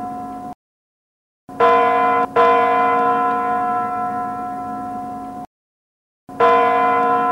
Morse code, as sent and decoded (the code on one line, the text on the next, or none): none